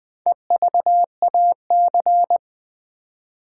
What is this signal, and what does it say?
Morse code, heard as a keyed tone, about 20 wpm, 700 Hz. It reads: EVAC
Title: Build A Word - 2 Letters At A Time - 8-Letter Words - 20wpm